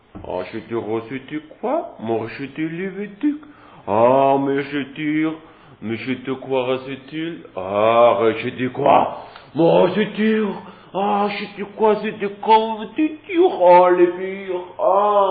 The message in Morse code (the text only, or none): none